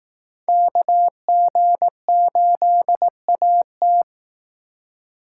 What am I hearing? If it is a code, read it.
Morse code, heard as a keyed tone, about 19 wpm, 700 Hz. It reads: KG8AT